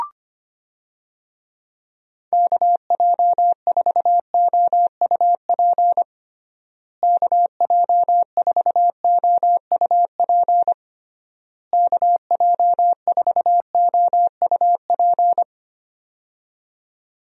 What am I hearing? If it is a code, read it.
Morse code, heard as a keyed tone, about 25 wpm, 700 Hz. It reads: KJ4OUP KJ4OUP KJ4OUP
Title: USA Call Signs - 6 Characters - Repeated 3x - 25wpm